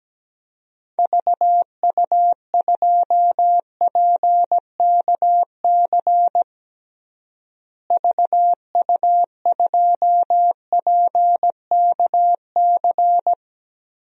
Morse code, heard as a keyed tone, about 17 wpm, 700 Hz. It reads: VU2PKC VU2PKC
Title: DXCC-100 Call Signs - 6 Characters - Repeated 3x - 17wpm